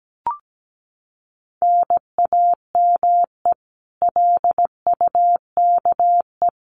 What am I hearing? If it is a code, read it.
Morse code, heard as a keyed tone, about 17 wpm, 700 Hz. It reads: NAME LUKE
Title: Common QSO Phrases 17wpm